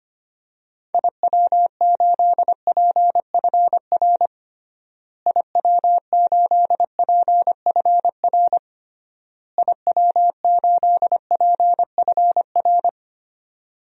Morse code, heard as a keyed tone, about 25 wpm, 700 Hz. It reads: IW8PFR IW8PFR IW8PFR
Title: DXCC-100 Call Signs - 6 Characters - Repeated 3x - 25wpm